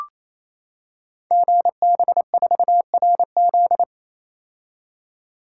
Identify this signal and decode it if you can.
Morse code, heard as a keyed tone, about 28 wpm, 700 Hz. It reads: GB4RZ
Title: Code Groups - 5 Letter:Numbers - 28wpm